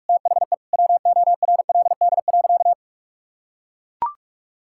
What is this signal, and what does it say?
Morse code, heard as a keyed tone, about 41 wpm, 700 Hz. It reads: THE WORLD.
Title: Sentences from Top 1000 Words - Review Part 6 - 45wpm